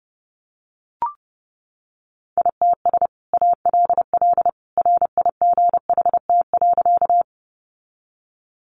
Morse code, heard as a keyed tone, about 30 wpm, 700 Hz. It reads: ITS ALL RIGHT.
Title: Sentences from Top 600 Words 30wpm